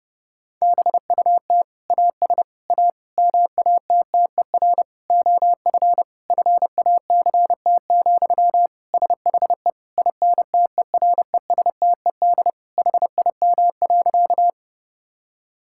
Morse code, heard as a keyed tone, about 30 wpm, 700 Hz. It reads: BUT AS A MATTER OF FACT, SHE INTERESTED HIM.
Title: Sentences from Top 1000 Words - Review Part 6 - 30wpm